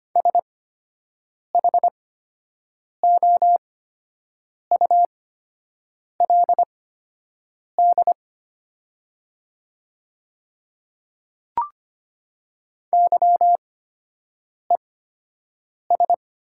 Morse code, heard as a keyed tone, about 25 wpm, spaced as for 7 wpm, 700 Hz. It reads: SHOULD YES
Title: 8x Character Spacing - Top 500 Words - 25wpm